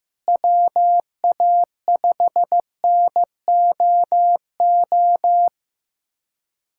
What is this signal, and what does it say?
Morse code, heard as a keyed tone, about 16 wpm, 700 Hz. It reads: WA5NOO
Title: USA Callsign DE Callsign - 6 Characters - 15wpm